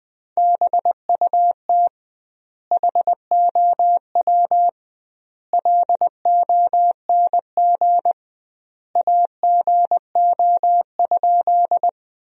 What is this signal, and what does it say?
Morse code, heard as a keyed tone, about 21 wpm, 700 Hz. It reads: BUT HOW LONG AGO?
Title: 2x Word-Spacing - Sentences from Top 500 Words - 20wpm